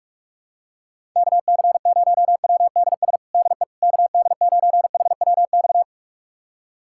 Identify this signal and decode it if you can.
Morse code, heard as a keyed tone, about 46 wpm, 700 Hz. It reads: KC0WDS DE KD9HWX
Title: USA Callsign DE Callsign - 6 Characters - 45wpm